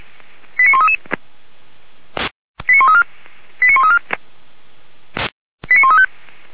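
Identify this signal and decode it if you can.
Five-tone selective call (ZVEI), heard as quick runs of five tones, tones of about 80 ms. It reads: 89133 89134 89134 89135
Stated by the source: ZVEI five-tone call 89132 received on SDR by feuerwehr-orlamuende.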